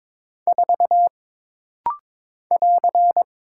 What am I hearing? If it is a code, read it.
Morse code, heard as a keyed tone, about 22 wpm, 700 Hz. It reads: 4 <AR>
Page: This rapid-fire timing is geared to push your ICR (Instant Character Recognition) skills.